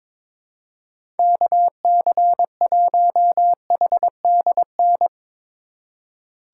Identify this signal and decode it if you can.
Morse code, heard as a keyed tone, about 22 wpm, 700 Hz. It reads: KC1HDN
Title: USA Call Signs - 6 Characters - 2x Character Spacing - 22wpm